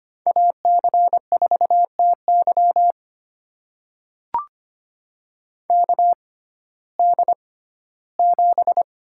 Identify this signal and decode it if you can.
Morse code, heard as a keyed tone, about 25 wpm, spaced as for 15 wpm, 700 Hz. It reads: AC4TY KD7
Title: USA Call Signs - 6x Character Spacing - 25wpm